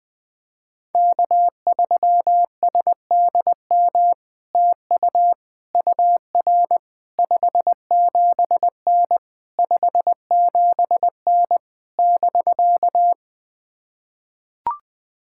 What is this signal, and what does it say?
Morse code, heard as a keyed tone, about 20 wpm, 700 Hz. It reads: K3SDM TU UR 57N 57N <BK>
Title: POTA QSO Practice - 20wpm